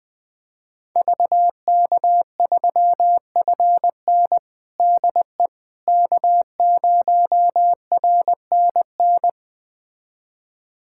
Morse code, heard as a keyed tone, about 20 wpm, 700 Hz. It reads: VK3FN DE K0RNN
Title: USA Callsign DE Callsign - 5 Characters - 20wpm